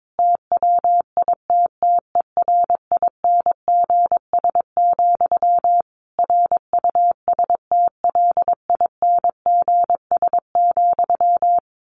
Morse code, heard as a keyed tone, about 22 wpm, 700 Hz. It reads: TWITTERINGS, RUSTLINGS,
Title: A Sound Of Thunder 22wpm